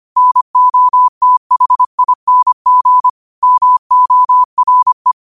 Morse code, heard as a keyed tone, about 25 wpm, 1000 Hz. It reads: NOTHING MORE
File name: qotdfiles / 25wpm